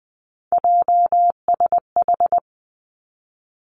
Morse code, heard as a keyed tone, about 19 wpm, 700 Hz. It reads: JSH
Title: Three Challenging Letter-Number Combinations 20wpm